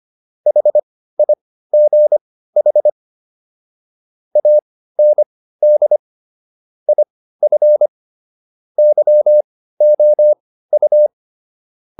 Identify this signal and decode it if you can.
Morse code, heard as a keyed tone, about 25 wpm, spaced as for 15 wpm, 600 Hz. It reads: HIGH AND IF YOU